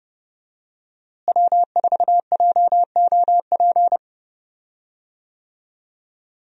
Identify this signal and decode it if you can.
Morse code, heard as a keyed tone, about 30 wpm, 700 Hz. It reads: W4JOP
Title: USA Call Signs - 5 Characters - Repeated 3x - 30wpm